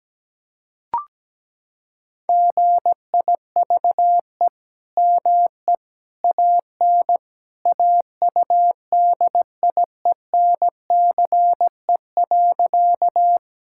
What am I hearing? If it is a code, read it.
Morse code, heard as a keyed tone, about 17 wpm, 700 Hz. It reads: GIVE ME AN AUDIENCE.
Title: Sentences from Top 1000 Words - Review Part 5 - 17wpm